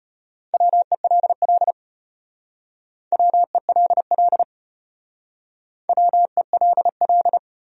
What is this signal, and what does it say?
Morse code, heard as a keyed tone, about 33 wpm, 700 Hz. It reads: WELL WELL WELL